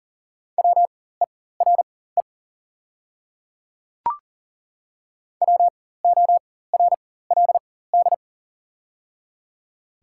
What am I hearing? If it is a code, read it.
Morse code, heard as a keyed tone, about 38 wpm, spaced as for 18 wpm, 690 Hz. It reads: WERE WORLD
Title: 4x Character Spacing - Top 200 Words - 40wpm